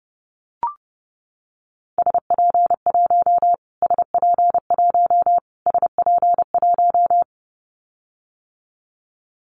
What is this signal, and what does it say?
Morse code, heard as a keyed tone, about 30 wpm, 700 Hz. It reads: SP1 SP1 SP1
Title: Callsign Prefixes for Top 25 DXCC - Repeated 3x - 30wpm